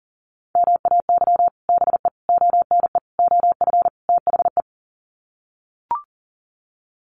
Morse code, heard as a keyed tone, about 40 wpm, 700 Hz. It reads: MAY BE ONE OF THE
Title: Sets of 5 Words Encore2 40wpm